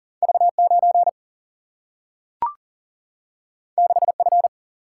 Morse code, heard as a keyed tone, about 40 wpm, 700 Hz. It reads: V9 6F
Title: Single-Letter Single-Number 40wpm